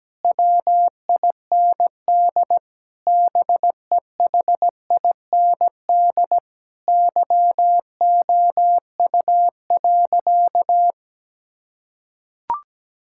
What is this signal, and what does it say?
Morse code, heard as a keyed tone, about 17 wpm, 700 Hz. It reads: WIND BEHIND YOU.
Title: Sentences from Top 1000 Words - Review Part 6 - 17wpm